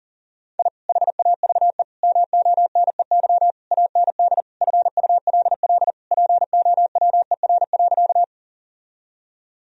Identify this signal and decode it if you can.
Morse code, heard as a keyed tone, about 40 wpm, 700 Hz. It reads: I HAVE MONEY AND FULL POWER.
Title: Sentences from Top 1000 Words - Review Part 3 - 40wpm